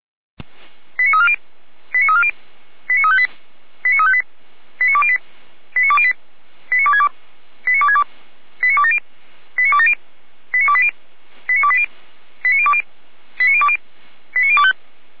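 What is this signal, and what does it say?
Five-tone selective call (ZVEI), heard as quick runs of five tones, tones of about 80 ms. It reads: 89340 89340 89358 89358 89208 89208 89262 89262 89270 89270 89280 89280 89020 89020 89025
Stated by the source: ZVEI five-tone selective call 89330 received by SDR.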